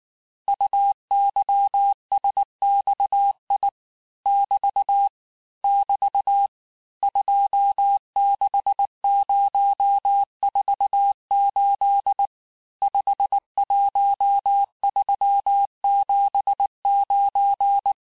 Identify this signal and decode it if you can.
Morse code, heard as a keyed tone, about 19 wpm, 800 Hz. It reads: UYSXI <BT> <BT> 26048 51379
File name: cours-du-lundi files / cours-1595-teste-CHIFFRE-vitesse18-MPM